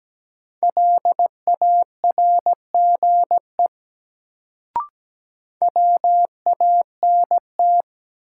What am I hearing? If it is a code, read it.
Morse code, heard as a keyed tone, about 17 wpm, 700 Hz. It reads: LARGE WANT
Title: Sentences from Top 200 Words - New Word Drill - 17wpm